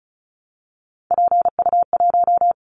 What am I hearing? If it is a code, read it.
Morse code, heard as a keyed tone, about 36 wpm, 690 Hz. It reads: PU1
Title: Callsign Prefixes for Top 50 DXCC - 35wpm